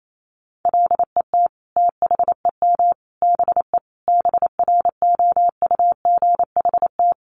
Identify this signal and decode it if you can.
Morse code, heard as a keyed tone, about 28 wpm, 700 Hz. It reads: LET THEM BE BROUGHT
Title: Sentences from Top 1300 Words 28wpm